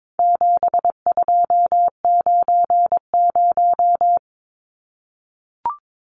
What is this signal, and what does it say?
Morse code, heard as a keyed tone, about 22 wpm, 700 Hz. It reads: 7290
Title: Numbers - 4-Digit 22wpm